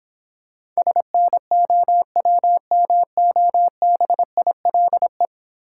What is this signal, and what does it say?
Morse code, heard as a keyed tone, about 25 wpm, 700 Hz. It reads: SNOWMOBILE
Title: Difficult Words - Speed-Racing with Hints - 20wpm